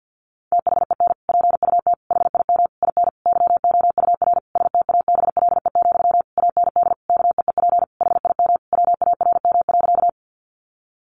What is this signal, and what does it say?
Morse code, heard as a keyed tone, about 50 wpm, 700 Hz. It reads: THEN PUT HIM IN YOUR STABLE, AND KEEP HIM WARM.
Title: Sentences from Top 1500 Words 50wpm